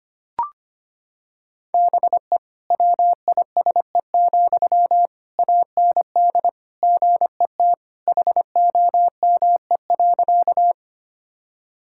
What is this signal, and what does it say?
Morse code, heard as a keyed tone, about 25 wpm, 700 Hz. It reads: BE WISE, AND GET HOME.
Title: Sentences from Top 1600 Words 25wpm